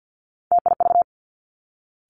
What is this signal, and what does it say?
Morse code, heard as a keyed tone, about 50 wpm, 700 Hz. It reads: TIV
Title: Three Challenging Letter-Number Combinations 50wpm